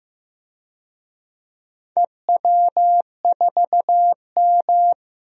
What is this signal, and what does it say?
Morse code, heard as a keyed tone, about 15 wpm, 700 Hz. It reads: EW4M
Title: DXCC-100 Call Signs - 4 Characters - 15wpm